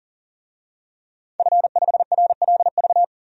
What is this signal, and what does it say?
Morse code, heard as a keyed tone, about 37 wpm, 700 Hz. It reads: F5RLV
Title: DXCC-100 Call Signs - 5 Characters - 40wpm